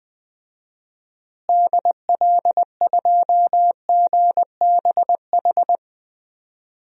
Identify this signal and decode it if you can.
Morse code, heard as a keyed tone, about 20 wpm, 700 Hz. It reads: DL2GBH
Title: DXCC-100 Call Signs - 6 Characters - Repeated 3x - 20wpm